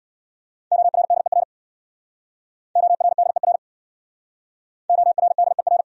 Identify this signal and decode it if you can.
Morse code, heard as a keyed tone, about 66 wpm, 700 Hz. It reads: ORDER ORDER ORDER